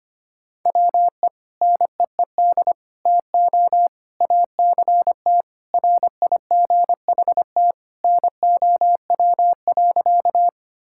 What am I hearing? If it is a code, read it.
Morse code, heard as a keyed tone, about 25 wpm, 700 Hz. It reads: WE NEED TO ACT RIGHT NOW.
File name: practice / 25wpm / Sentences with 2-Word Phrases - 25wpm